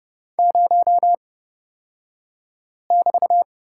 Morse code, heard as a keyed tone, about 30 wpm, spaced as for 5 wpm, 700 Hz. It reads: 0<BT>
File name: Lesson 37d - Letter x - Two Characters 30wpm